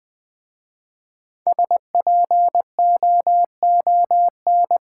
Morse code, heard as a keyed tone, about 20 wpm, 700 Hz. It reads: SPOON
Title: Easy Words with Hints - 20wpm